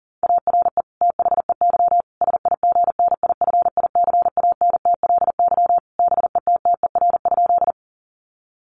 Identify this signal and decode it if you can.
Morse code, heard as a keyed tone, about 40 wpm, 700 Hz. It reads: ARE THEY SIGNIFICANTLY BETTER?
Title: Sentences from Top 1500 Words 40wpm